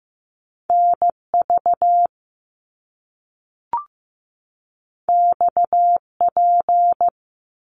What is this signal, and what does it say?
Morse code, heard as a keyed tone, about 15 wpm, 700 Hz. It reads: NV XP
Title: Top Two Letter Combinations in Words 15wpm